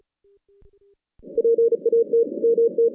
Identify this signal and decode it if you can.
Morse code, heard as a keyed tone, about 36 wpm, 460 Hz. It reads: PAT MT
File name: 2016 JAN CW NAQP / 2016 Jan naqp k5ot - 0-10